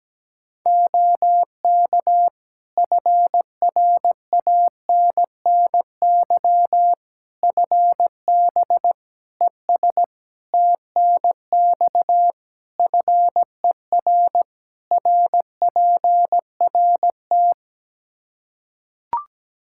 Morse code, heard as a keyed tone, about 17 wpm, 700 Hz. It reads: OK FRANNY FB ES TNX FER RPRT
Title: Practice QSO - Exchange 2 - Info - 17wpm